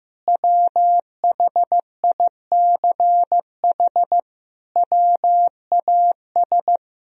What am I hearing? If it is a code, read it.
Morse code, heard as a keyed tone, about 16 wpm, 700 Hz. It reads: WHICH WAS
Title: Sentences from Top 1000 Words - Review Part 3 - 15wpm